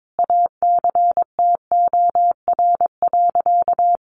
Morse code, heard as a keyed tone, about 22 wpm, 700 Hz. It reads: ACTOR.
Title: Sentences from Top 1500 Words 22wpm